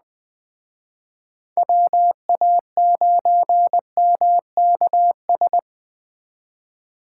Morse code, heard as a keyed tone, about 21 wpm, 700 Hz. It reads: WA9MKS